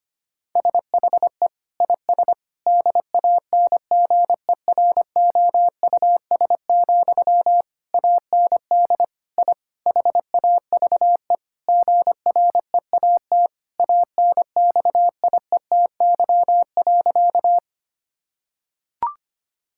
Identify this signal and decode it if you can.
Morse code, heard as a keyed tone, about 25 wpm, 700 Hz. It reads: SHE IS DANGEROUS, AND I HAVE GREAT ANXIETY.
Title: Sentences from Top 1600 Words 25wpm